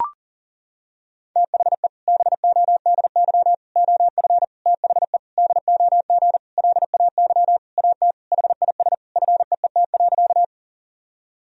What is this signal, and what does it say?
Morse code, heard as a keyed tone, about 40 wpm, 700 Hz. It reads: THE BODY OF THE DOG LAY AT HIS FEET.
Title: Sentences from Top 1000 Words - Review Part 2 - 40wpm